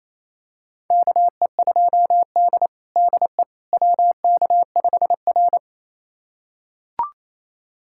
Morse code, heard as a keyed tone, about 28 wpm, 700 Hz. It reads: KE2D DE WK5R
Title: USA Callsign DE Callsign - 4 Characters - 28wpm